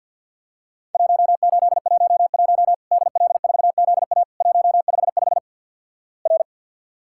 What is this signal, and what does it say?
Morse code, heard as a keyed tone, about 53 wpm, 700 Hz. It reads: 1811 DL4ZA 155 R
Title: Worked All Europe DX Contest - QTC Exchange - 50wpm